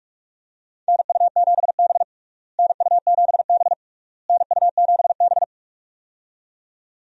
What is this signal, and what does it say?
Morse code, heard as a keyed tone, about 42 wpm, 700 Hz. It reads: NU7B NU7B NU7B
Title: USA Call Signs - 4 Characters - Repeated 3x - 45wpm